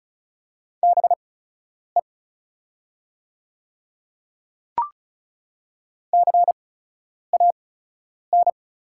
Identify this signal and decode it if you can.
Morse code, heard as a keyed tone, about 36 wpm, spaced as for 10 wpm, 700 Hz. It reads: BE CAN
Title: 8x Character Spacing - Top 100 Words - 35wpm